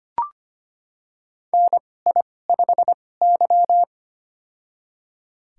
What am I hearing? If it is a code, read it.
Morse code, heard as a keyed tone, about 25 wpm, spaced as for 18 wpm, 700 Hz. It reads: NI5Y
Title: USA Call Signs - 4 Characters - 2x Character Spacing - 25wpm